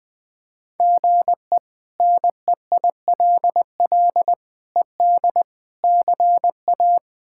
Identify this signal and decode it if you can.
Morse code, heard as a keyed tone, about 20 wpm, 700 Hz. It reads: GE NEILL ED CA
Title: K1USN SST QSO Practice - 20wpm